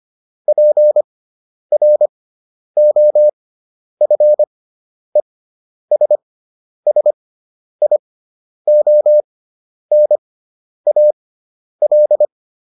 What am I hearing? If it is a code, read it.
Morse code, heard as a keyed tone, about 25 wpm, spaced as for 10 wpm, 600 Hz. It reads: PROFESSIONAL